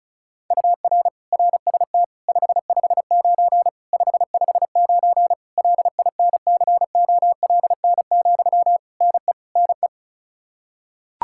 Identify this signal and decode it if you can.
Morse code, heard as a keyed tone, about 34 wpm, 700 Hz. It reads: UR RST 559 559 LINCOLN, NE NE
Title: Practice QSO - Exchange 1 - Beginning - 35wpm